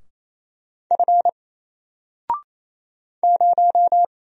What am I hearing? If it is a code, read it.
Morse code, heard as a keyed tone, about 27 wpm, 700 Hz. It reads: F 0
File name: Single-Letter-Number-28wpm